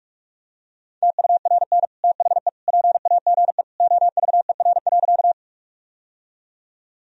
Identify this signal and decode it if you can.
Morse code, heard as a keyed tone, about 46 wpm, 700 Hz. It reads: TURN THE PAGE OVER.
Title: Sentences from Top 1000 Words - Review Part 1 - 45wpm